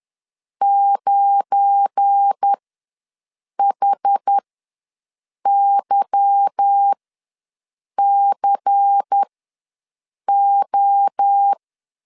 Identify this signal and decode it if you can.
Morse code, heard as a keyed tone, about 10 wpm, spaced as for 6 wpm, 800 Hz. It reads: 9HYCO